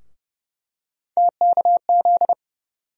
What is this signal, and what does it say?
Morse code, heard as a keyed tone, about 30 wpm, 700 Hz. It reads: TKZ
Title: USA Call Sign Postfixes - 3 Letters 30wpm